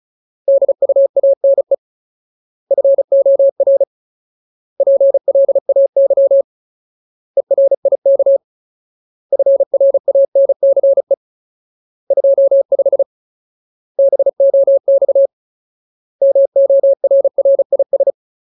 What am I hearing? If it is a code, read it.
Morse code, heard as a keyed tone, about 34 wpm, 550 Hz. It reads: DUANE FOR PLAY ERIK FRANCE 25 BOX MORRIS
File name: Ultimate Word List / 35WPM Ultimate